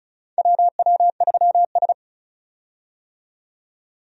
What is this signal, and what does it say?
Morse code, heard as a keyed tone, about 35 wpm, 700 Hz. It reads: WW3S
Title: USA Call Signs - 4 Characters - 35wpm